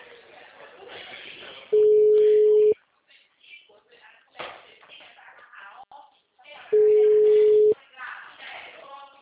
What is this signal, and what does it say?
Ringback tone (Europe, 425 Hz), with speech.